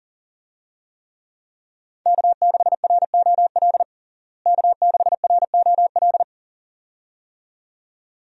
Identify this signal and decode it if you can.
Morse code, heard as a keyed tone, about 40 wpm, 700 Hz. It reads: K6ROL K6ROL
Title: USA Call Signs - 5 Characters - Repeated 2x - 40wpm